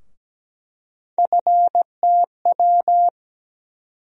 Morse code, heard as a keyed tone, about 17 wpm, 700 Hz. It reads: FTW
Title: USA Call Sign Postfixes - 3 Letters 17wpm